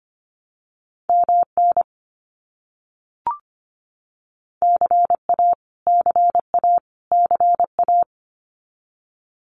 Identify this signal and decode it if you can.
Morse code, heard as a keyed tone, about 25 wpm, 700 Hz. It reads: MN CA CA CA
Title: US States Repeated 3x 25wpm